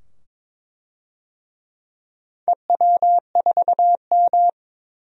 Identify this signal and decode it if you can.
Morse code, heard as a keyed tone, about 21 wpm, 700 Hz. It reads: EW4M
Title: DXCC-100 Call Signs - 4 Characters - 22wpm